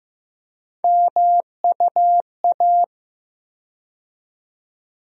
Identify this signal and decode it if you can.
Morse code, heard as a keyed tone, about 15 wpm, 700 Hz. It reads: MUA
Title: USA Call Sign Postfixes - 3 Letters 15wpm